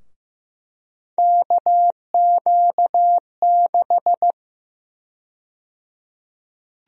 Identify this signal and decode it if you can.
Morse code, heard as a keyed tone, about 16 wpm, 700 Hz. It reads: KQ6
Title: USA Call Sign Prefixes - Repeated 2x - 15wpm